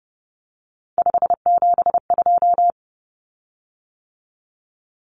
Morse code, heard as a keyed tone, about 30 wpm, 700 Hz. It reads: 572
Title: Common Cut Numbers - 3-Digit 30wpm